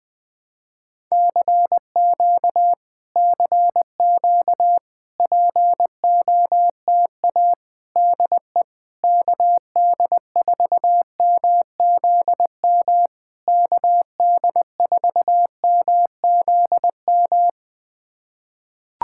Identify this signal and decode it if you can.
Morse code, heard as a keyed tone, about 21 wpm, 700 Hz. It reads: CQ CQ POTA DE KD4MZM KD4MZM
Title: POTA QSO Practice - 20wpm